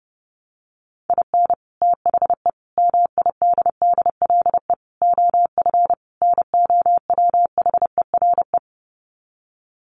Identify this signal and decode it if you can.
Morse code, heard as a keyed tone, about 30 wpm, 700 Hz. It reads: IN THE MIDDLE OF NOWHERE
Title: Sets of 5 Words Encore 30wpm